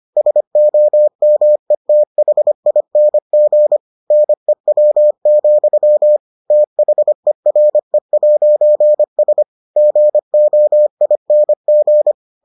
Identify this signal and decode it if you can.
Morse code, heard as a keyed tone, about 25 wpm, 600 Hz. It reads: SOMETHING NEW, THERE'S GOING